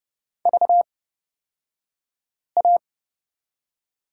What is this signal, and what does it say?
Morse code, heard as a keyed tone, about 30 wpm, spaced as for 5 wpm, 700 Hz. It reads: VA
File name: Lesson 38d - Letter q - Two Characters 30wpm